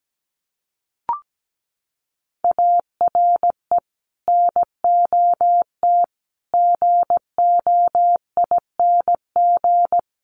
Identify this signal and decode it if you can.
Morse code, heard as a keyed tone, about 17 wpm, 700 Hz. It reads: ARE NOT GOING
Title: Sets of 4 Words 17wpm